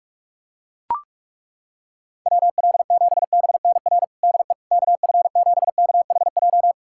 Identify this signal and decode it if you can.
Morse code, heard as a keyed tone, about 42 wpm, 690 Hz. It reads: WL7BNR DE KF7KHJ
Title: USA Callsign DE Callsign - 6 Characters - 45wpm